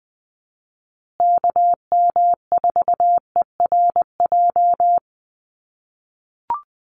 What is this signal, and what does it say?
Morse code, heard as a keyed tone, about 20 wpm, 700 Hz. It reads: KM4ERJ
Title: USA Call Signs Repeated 2x 20wpm